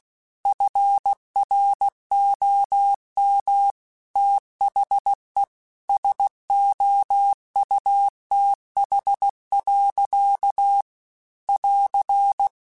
Morse code, heard as a keyed tone, about 16 wpm, 800 Hz. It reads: FROM THE SOUTH. <AR>